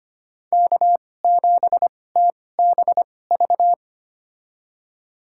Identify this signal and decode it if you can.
Morse code, heard as a keyed tone, about 25 wpm, spaced as for 18 wpm, 700 Hz. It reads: K7TBV
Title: USA Call Signs - 5 Characters - 2x Character Spacing - 25wpm